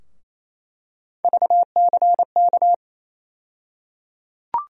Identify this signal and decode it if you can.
Morse code, heard as a keyed tone, about 28 wpm, 700 Hz. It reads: VCK
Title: USA Call Sign Postfixes - 3 Letters 28wpm